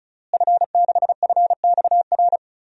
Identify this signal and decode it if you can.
Morse code, heard as a keyed tone, about 36 wpm, 700 Hz. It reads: F6FXR
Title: DXCC-100 Call Signs - 5 Characters - Repeated 2x - 35wpm